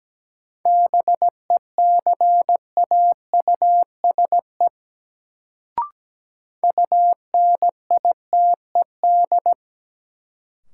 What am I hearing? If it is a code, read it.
Morse code, heard as a keyed tone, about 17 wpm, 700 Hz. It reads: BECAUSE UNITED
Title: Sentences from Top 600 Words - New Word Drill - 17wpm